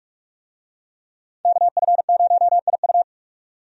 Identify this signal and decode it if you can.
Morse code, heard as a keyed tone, about 46 wpm, 700 Hz. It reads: KF0IU
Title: USA Call Signs - 5 Characters - 45wpm